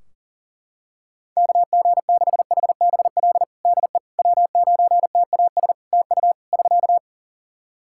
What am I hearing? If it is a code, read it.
Morse code, heard as a keyed tone, about 40 wpm, 700 Hz. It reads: KG6HBL DE W9TAS TU <SK>
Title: Practice QSO - Exchange 4 - Ending - 40wpm